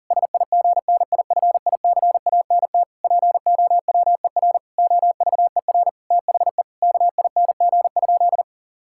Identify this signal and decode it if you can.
Morse code, heard as a keyed tone, about 40 wpm, 690 Hz. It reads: SIGNIFICANT POWER OVER THE KING?
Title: Sentences from Top 1000 Words - Review Part 6 - 40wpm